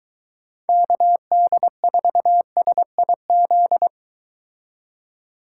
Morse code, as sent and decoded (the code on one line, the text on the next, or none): -.- -.. ....- ... .. --..
KD4SIZ